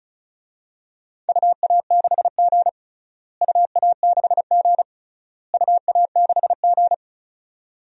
Morse code, heard as a keyed tone, about 36 wpm, 700 Hz. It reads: UA6G UA6G UA6G
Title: DXCC-100 Call Signs - 4 Characters - Repeated 3x - 35wpm